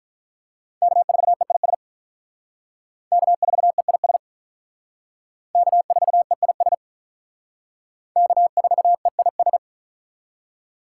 Morse code, heard as a keyed tone, about 45 wpm, 700 Hz. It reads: K4EIS K4EIS K4EIS K4EIS